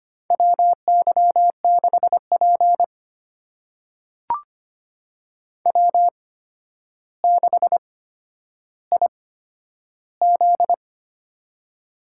Morse code, heard as a keyed tone, about 25 wpm, spaced as for 11 wpm, 700 Hz. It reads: WY6P W6IZ
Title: USA Call Signs - 4 Characters - 8x Character Spacing - 25wpm